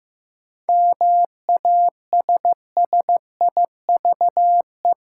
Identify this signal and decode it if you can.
Morse code, heard as a keyed tone, about 15 wpm, 700 Hz. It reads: MASSIVE
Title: Seven Letter Words - 15wpm